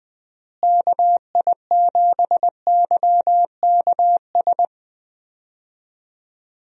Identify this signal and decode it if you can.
Morse code, heard as a keyed tone, about 20 wpm, 700 Hz. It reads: KI7YKS
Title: USA Call Signs - 6 Characters - Repeated 3x - 20wpm